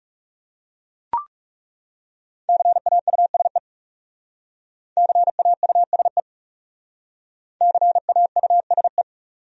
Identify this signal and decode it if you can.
Morse code, heard as a keyed tone, about 39 wpm, 700 Hz. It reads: CAUSE CAUSE CAUSE